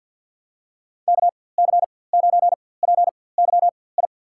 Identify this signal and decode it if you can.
Morse code, heard as a keyed tone, about 51 wpm, spaced as for 23 wpm, 700 Hz. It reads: KC9PYI
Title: USA Call Signs - 4x Character Spacing - 50wpm